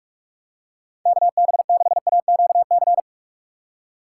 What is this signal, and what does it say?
Morse code, heard as a keyed tone, about 42 wpm, 700 Hz. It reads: KB6AQC
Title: USA Call Signs - 6 Characters - 6x Character Spacing - 45wpm